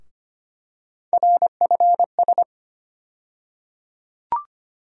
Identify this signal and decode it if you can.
Morse code, heard as a keyed tone, about 25 wpm, 700 Hz. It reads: RFS
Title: USA Call Sign Postfixes - 3 Letters 25wpm